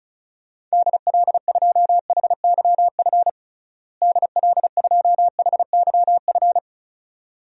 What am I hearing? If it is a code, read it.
Morse code, heard as a keyed tone, about 36 wpm, 700 Hz. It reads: DL2HYF DL2HYF
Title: DXCC-100 Call Signs - 6 Characters - Repeated 2x - 35wpm